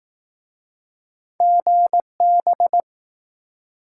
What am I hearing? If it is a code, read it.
Morse code, heard as a keyed tone, about 20 wpm, 700 Hz. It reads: GB